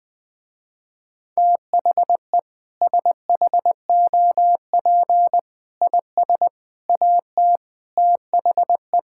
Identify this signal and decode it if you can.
Morse code, heard as a keyed tone, about 21 wpm, 700 Hz. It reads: THE SHOP IS AT THE
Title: Sentences with 5-Word Phrases - Spoken Beginning - 20wpm